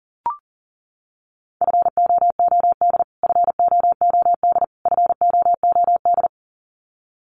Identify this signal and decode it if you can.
Morse code, heard as a keyed tone, about 40 wpm, 700 Hz. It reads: FOOD FOOD FOOD
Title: Top 500 Words Repeated 40wpm